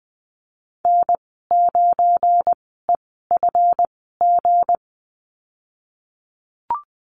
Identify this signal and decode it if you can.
Morse code, heard as a keyed tone, about 20 wpm, spaced as for 15 wpm, 700 Hz. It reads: N9EFG
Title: 193 Lesson 31g - Letter v - Call Signs 20wpm